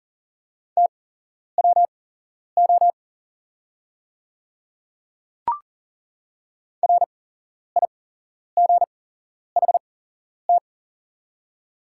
Morse code, heard as a keyed tone, about 40 wpm, spaced as for 11 wpm, 690 Hz. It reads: TWO RIGHT
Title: 8x Character Spacing - Top 300 Words - 40wpm